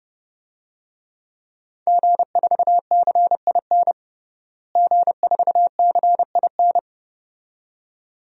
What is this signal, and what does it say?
Morse code, heard as a keyed tone, about 30 wpm, 700 Hz. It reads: G4CIN G4CIN
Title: DXCC-100 Call Signs - 5 Characters - Repeated 2x - 30wpm